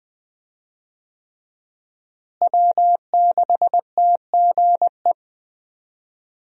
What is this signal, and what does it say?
Morse code, heard as a keyed tone, about 20 wpm, 700 Hz. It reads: W6TGE
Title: USA Call Signs - 5 Characters - Repeated 3x - 20wpm